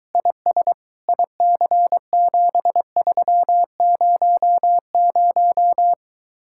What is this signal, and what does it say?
Morse code, heard as a keyed tone, about 23 wpm, 700 Hz. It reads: IS IC7300